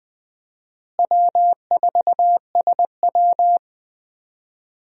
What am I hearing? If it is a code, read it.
Morse code, heard as a keyed tone, about 20 wpm, 700 Hz. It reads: W4SW
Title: USA Call Signs - 4 Characters - Repeated 3x - 20wpm